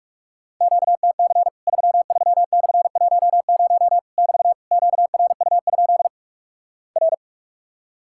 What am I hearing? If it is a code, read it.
Morse code, heard as a keyed tone, about 46 wpm, 700 Hz. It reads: QTC 33/10 <BT> QRU? R
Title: Worked All Europe DX Contest - QTC Exchange - 45wpm